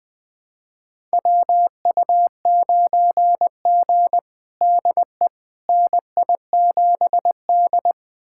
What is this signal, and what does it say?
Morse code, heard as a keyed tone, about 20 wpm, 700 Hz. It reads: WU9G DE NI7D
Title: USA Callsign DE Callsign - 4 Characters - 20wpm